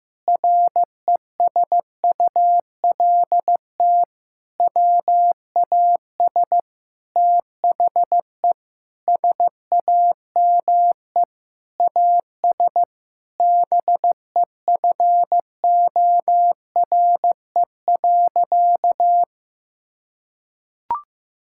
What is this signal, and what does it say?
Morse code, heard as a keyed tone, about 15 wpm, 700 Hz. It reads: RESULT WAS THE SAME AS BEFORE.
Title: Sentences from Top 1000 Words - Review Part 5 - 15wpm